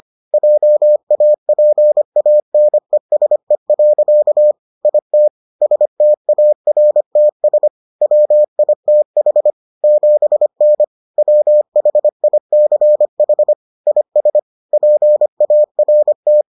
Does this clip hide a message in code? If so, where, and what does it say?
Yes: Morse code from 0.3 to 16.4 s, heard as a keyed tone, about 25 wpm, 600 Hz. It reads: JAPANESE. IT STARTS WITH 7N WHICH IS PART